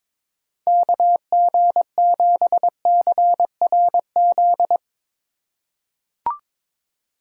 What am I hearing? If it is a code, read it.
Morse code, heard as a keyed tone, about 22 wpm, 700 Hz. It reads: KG7CRZ
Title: USA Call Signs - 3x Character Spacing - 22wpm